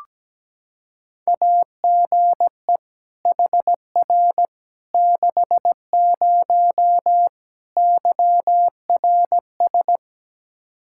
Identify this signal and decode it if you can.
Morse code, heard as a keyed tone, about 17 wpm, 700 Hz. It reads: AGE HR 60 YRS
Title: Practice QSO - Exchange 3 - More Info - 17wpm